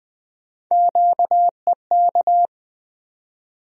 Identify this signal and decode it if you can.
Morse code, heard as a keyed tone, about 20 wpm, 700 Hz. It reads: QEK
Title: USA Call Sign Postfixes - 3 Letters 20wpm